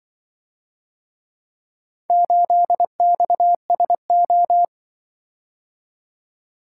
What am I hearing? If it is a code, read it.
Morse code, heard as a keyed tone, about 25 wpm, 700 Hz. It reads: 8XSO